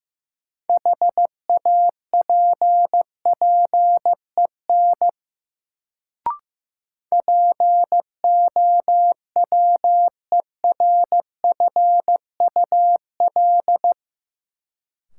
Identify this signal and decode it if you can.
Morse code, heard as a keyed tone, about 15 wpm, 700 Hz. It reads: HAPPEN POWERFUL
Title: Sets of 5 Words Encore - New Word Drill - 15wpm